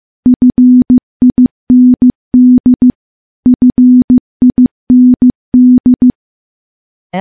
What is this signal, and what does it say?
Morse code, heard as a keyed tone, about 15 wpm, 250 Hz. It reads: FIND FIND
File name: morse / morse-training-15WPM